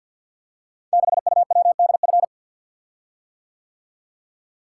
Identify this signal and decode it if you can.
Morse code, heard as a keyed tone, about 49 wpm, 700 Hz. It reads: 6UWDF
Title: Code Groups - 5 Letter:Numbers - 50wpm